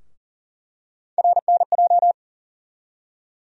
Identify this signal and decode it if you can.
Morse code, heard as a keyed tone, about 40 wpm, 700 Hz. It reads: RNJ
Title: USA Call Sign Postfixes - 3 Letters 40wpm